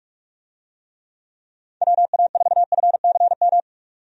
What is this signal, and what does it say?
Morse code, heard as a keyed tone, about 46 wpm, 700 Hz. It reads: WA4FCM
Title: USA Call Signs - 6 Characters - 45wpm